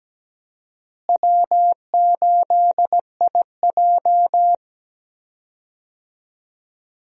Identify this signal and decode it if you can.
Morse code, heard as a keyed tone, about 17 wpm, 700 Hz. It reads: W8IJ
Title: USA Call Signs - 4 Characters - Repeated 2x - 17wpm